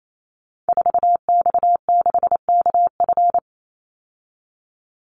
Morse code, heard as a keyed tone, about 28 wpm, 700 Hz. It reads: 4X6KF
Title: Call Signs Repeated 3x - 28wpm